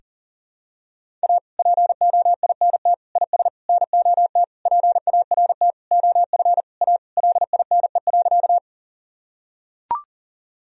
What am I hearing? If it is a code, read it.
Morse code, heard as a keyed tone, about 40 wpm, 700 Hz. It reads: A POINT IS NOT PART OF A LINE.
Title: Sentences from Top 1000 Words - Review Part 1 - 40wpm